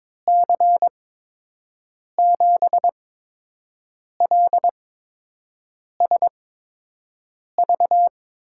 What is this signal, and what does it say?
Morse code, heard as a keyed tone, about 21 wpm, spaced as for 6 wpm, 700 Hz. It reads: C7LSV
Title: USA Call Signs - 6 Characters - 8x Character Spacing - 22wpm